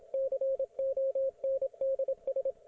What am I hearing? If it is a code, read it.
Morse code, heard as a keyed tone, about 27 wpm, 540 Hz, with radio noise. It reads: CONDS